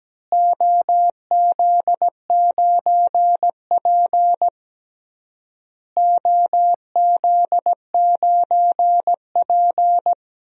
Morse code, heard as a keyed tone, about 17 wpm, 700 Hz. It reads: OZ9P OZ9P
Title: DXCC-100 Call Signs - 4 Characters - Repeated 3x - 17wpm